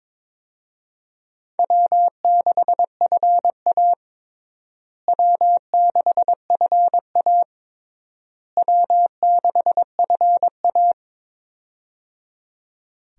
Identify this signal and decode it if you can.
Morse code, heard as a keyed tone, about 22 wpm, 700 Hz. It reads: W6FA W6FA W6FA
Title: USA Call Signs - 4 Characters - Repeated 3x - 22wpm